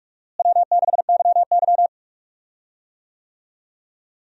Morse code, heard as a keyed tone, about 47 wpm, 700 Hz. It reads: W6YY
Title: USA Call Signs - 4 Characters - Repeated 2x - 45wpm